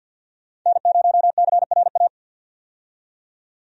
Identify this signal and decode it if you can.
Morse code, heard as a keyed tone, about 48 wpm, 700 Hz. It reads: N0CRA
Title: USA Call Signs - 5 Characters - 50wpm